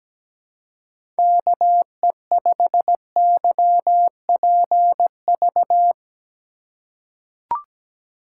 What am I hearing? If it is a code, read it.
Morse code, heard as a keyed tone, about 17 wpm, 700 Hz. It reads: KE5YPV
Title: USA Call Signs - 6 Characters - 3x Character Spacing - 17wpm